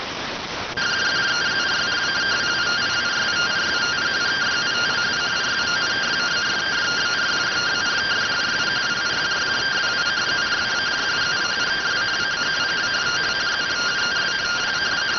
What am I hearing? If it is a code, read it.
Two-tone teleprinter signal, about 170 Hz shift.